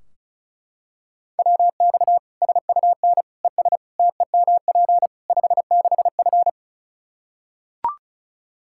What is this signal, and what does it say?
Morse code, heard as a keyed tone, about 36 wpm, 700 Hz. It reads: WX SUN ES TEMP 56F
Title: Practice QSO - Exchange 2 - Info - 35wpm